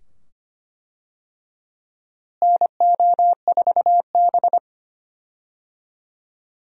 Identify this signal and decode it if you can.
Morse code, heard as a keyed tone, about 25 wpm, 700 Hz. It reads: NO4B
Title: USA Call Signs - 4 Characters - Repeated 2x - 25wpm